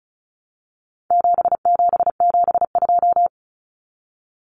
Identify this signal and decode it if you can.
Morse code, heard as a keyed tone, about 34 wpm, 700 Hz. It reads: 7772
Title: Numbers - 4-Digit 35wpm